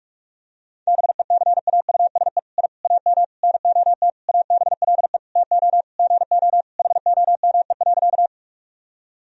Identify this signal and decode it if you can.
Morse code, heard as a keyed tone, about 47 wpm, 700 Hz. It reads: BECAUSE I AM NOT ABLE TO GO HOME.
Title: Sentences from Top 1000 Words - Review Part 3 - 45wpm